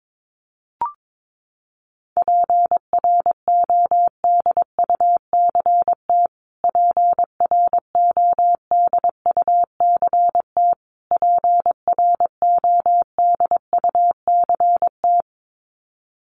Morse code, heard as a keyed tone, about 22 wpm, 700 Hz. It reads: PRODUCT PRODUCT PRODUCT
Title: Top 500 Words Repeated 22wpm